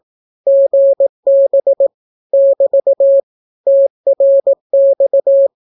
Morse code, heard as a keyed tone, about 18 wpm, 550 Hz. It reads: GB <BT> TRX
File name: naukapodstaw / skroty / skroty